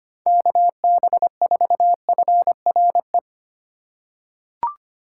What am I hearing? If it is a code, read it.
Morse code, heard as a keyed tone, about 25 wpm, 700 Hz. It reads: KB4FRE
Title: USA Call Signs - 6 Characters - 3x Character Spacing - 25wpm